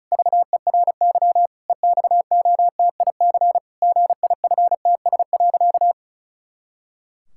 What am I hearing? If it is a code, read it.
Morse code, heard as a keyed tone, about 34 wpm, 700 Hz. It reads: VERY EXOTIC GIFTS.
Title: Pangram Challenge v2 - 35wpm